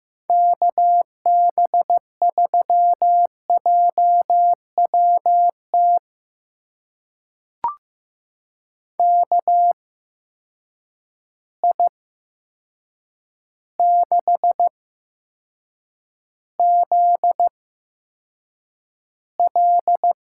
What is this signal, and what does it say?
Morse code, heard as a keyed tone, about 15 wpm, spaced as for 7 wpm, 700 Hz. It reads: KB3JWT KI6ZL
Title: USA Call Signs - 6 Characters - 8x Character Spacing - 15wpm